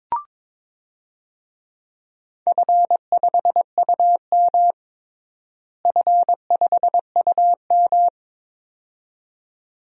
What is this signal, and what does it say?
Morse code, heard as a keyed tone, about 22 wpm, 700 Hz. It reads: F5UM F5UM
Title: DXCC-100 Call Signs - 4 Characters - Repeated 2x - 22wpm